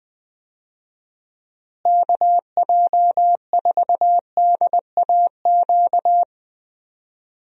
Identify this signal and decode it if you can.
Morse code, heard as a keyed tone, about 20 wpm, 700 Hz. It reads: KJ4DAQ
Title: USA Call Signs - 6 Characters - Repeated 3x - 20wpm